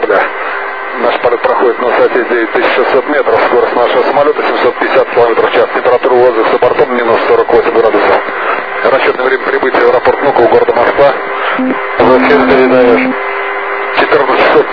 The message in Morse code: none

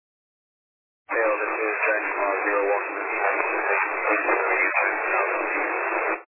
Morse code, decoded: none

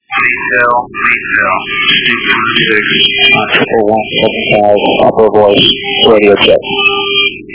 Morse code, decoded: none

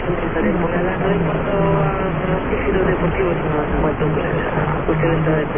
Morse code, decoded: none

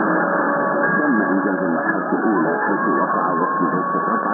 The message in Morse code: none

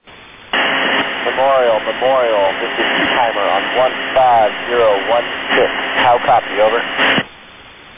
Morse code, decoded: none

